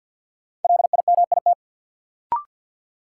Morse code, heard as a keyed tone, about 52 wpm, 700 Hz. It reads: LIMIT
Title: Sentences from Top 1500 Words - New Word Drill Alt - 50wpm